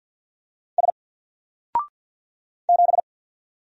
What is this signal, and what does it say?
Morse code, heard as a keyed tone, about 48 wpm, 700 Hz. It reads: S 7
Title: Single Letter-Number 50wpm